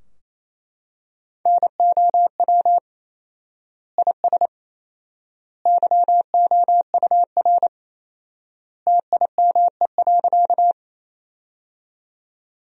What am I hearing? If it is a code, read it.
Morse code, heard as a keyed tone, about 28 wpm, 700 Hz. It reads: NOW IS YOUR TIME.
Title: 4x Word-Spacing - Sentences from Top 100 Words - 28wpm